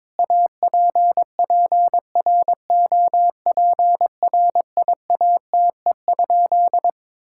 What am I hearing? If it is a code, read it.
Morse code, heard as a keyed tone, about 22 wpm, 700 Hz. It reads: APPROPRIATE?
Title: Sentences from Top 1000 Words - Review Part 5 - 22wpm